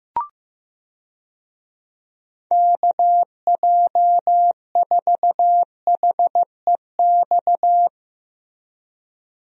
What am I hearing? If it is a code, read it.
Morse code, heard as a keyed tone, about 15 wpm, 700 Hz. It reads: KJ4HEX
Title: USA Call Signs - 6 Characters - Repeated 2x - 15wpm